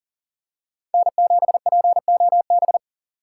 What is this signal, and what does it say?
Morse code, heard as a keyed tone, about 40 wpm, 700 Hz. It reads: N7POB
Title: USA Call Signs - 5 Characters - 6x Character Spacing - 40wpm